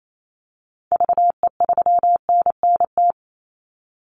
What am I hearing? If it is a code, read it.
Morse code, heard as a keyed tone, about 28 wpm, 700 Hz. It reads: VE3NNT
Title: Call Signs - 28wpm